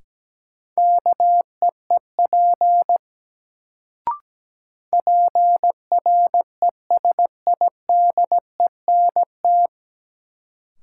Morse code, heard as a keyed tone, about 17 wpm, 700 Hz. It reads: KEEP PRESIDENT
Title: Sets of 2 Words - New Word Drill - 17wpm